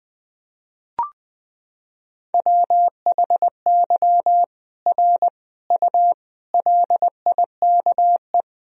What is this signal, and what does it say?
Morse code, heard as a keyed tone, about 21 wpm, 700 Hz. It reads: WHY R U LIKE
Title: Sentences from Top 200 Words and Abbreviations 20wpm